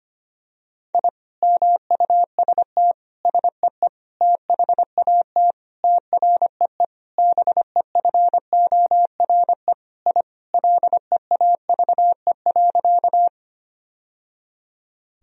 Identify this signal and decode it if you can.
Morse code, heard as a keyed tone, about 25 wpm, 700 Hz. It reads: I MUST SEE THAT TREE BEFORE I LEAVE.
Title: Sentences from Top 1000 Words - Review Part 1 - 25wpm